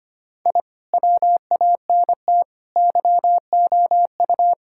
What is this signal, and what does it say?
Morse code, heard as a keyed tone, about 25 wpm, 700 Hz. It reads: I WANT YOU
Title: Sentences from Top 1000 Words - Review Part 6 - 25wpm